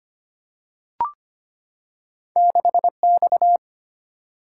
Morse code, heard as a keyed tone, about 25 wpm, 700 Hz. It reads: 6 X
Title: Single-Letter Single-Number 25wpm